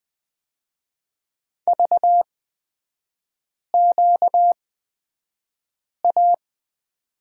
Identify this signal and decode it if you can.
Morse code, heard as a keyed tone, about 20 wpm, spaced as for 5 wpm, 690 Hz. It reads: VQA
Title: 236 Lesson 38e - Letter q - Three Characters 20wpm